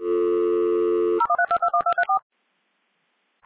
DTMF, heard as telephone touch tones, tones of about 80 ms. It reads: 013221234